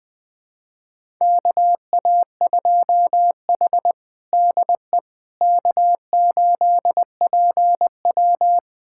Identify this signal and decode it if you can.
Morse code, heard as a keyed tone, about 21 wpm, 700 Hz. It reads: KA2H DE K8PW
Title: USA Callsign DE Callsign - 4 Characters - 20wpm